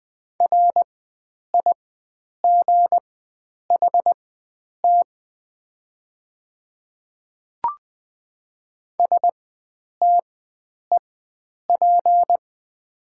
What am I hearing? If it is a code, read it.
Morse code, heard as a keyed tone, about 20 wpm, spaced as for 9 wpm, 700 Hz. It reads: RIGHT STEP
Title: 4x Character Spacing - Top 500 Words - 20wpm